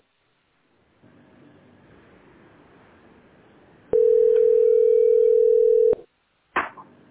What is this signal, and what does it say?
Ringback tone (North America), 440 + 480 Hz.